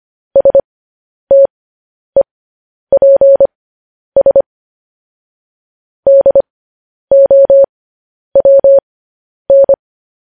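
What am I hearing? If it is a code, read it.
Morse code, heard as a keyed tone, about 25 wpm, spaced as for 10 wpm, 550 Hz. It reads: STEPS DOWN
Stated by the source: Morse news headlines sent at 10 wpm.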